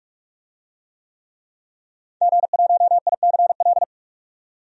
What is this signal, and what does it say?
Morse code, heard as a keyed tone, about 42 wpm, 700 Hz. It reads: G1ICL
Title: DXCC-100 Call Signs - 5 Characters - 45wpm